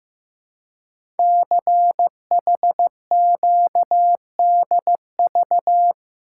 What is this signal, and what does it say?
Morse code, heard as a keyed tone, about 15 wpm, 700 Hz. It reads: CHQDV
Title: Code Groups - 5 Letter:Numbers - 15wpm